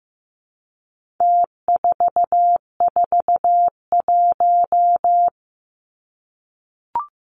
Morse code, heard as a keyed tone, about 15 wpm, 700 Hz. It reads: T441
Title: Numbers - Contest Style - 4-Digit 15wpm